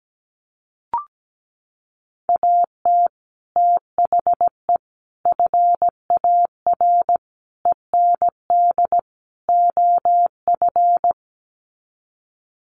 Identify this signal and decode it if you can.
Morse code, heard as a keyed tone, about 17 wpm, 700 Hz. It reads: AT THE FAR END OF
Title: Sets of 5 Words 17wpm